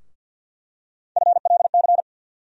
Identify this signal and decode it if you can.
Morse code, heard as a keyed tone, about 50 wpm, 700 Hz. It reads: FLC
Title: USA Call Sign Postfixes - 3 Letters 50wpm